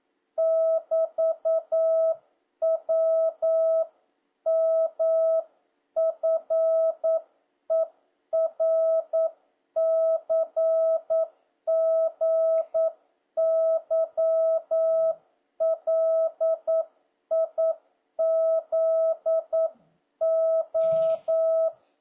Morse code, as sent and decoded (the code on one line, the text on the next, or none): -...- .-- -- ..-. .-. -.-. --. -.-- .-.. .. --.. ---
<BT>WMFERCGYLIZO